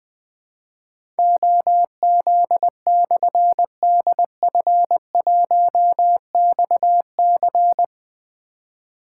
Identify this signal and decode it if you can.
Morse code, heard as a keyed tone, about 20 wpm, 700 Hz. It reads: OZ/DF1XC
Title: Difficult Call Signs 20wpm